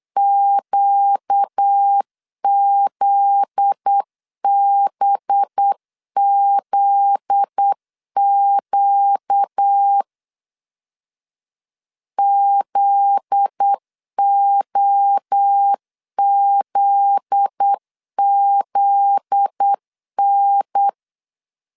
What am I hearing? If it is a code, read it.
Morse code, heard as a keyed tone, about 8 wpm, 800 Hz. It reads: QZBZQ ZOZZN
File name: LEZIONI / 7 GF9Z / 7 GF9Z 8wpm 3213